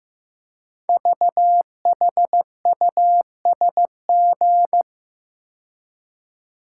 Morse code, heard as a keyed tone, about 15 wpm, 700 Hz. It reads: VHUSG
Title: Code Groups - 5 Letters - 15wpm